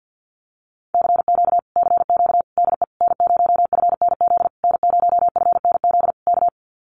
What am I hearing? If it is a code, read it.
Morse code, heard as a keyed tone, about 48 wpm, 700 Hz. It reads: CQ CQ DE N0FNZ N0FNZ K
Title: Common QSO Phrases Encore 50wpm